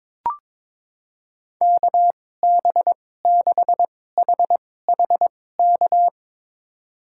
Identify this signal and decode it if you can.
Morse code, heard as a keyed tone, about 22 wpm, spaced as for 16 wpm, 700 Hz. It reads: KB6HHK
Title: USA Call Signs - 6 Characters - 2x Character Spacing - 22wpm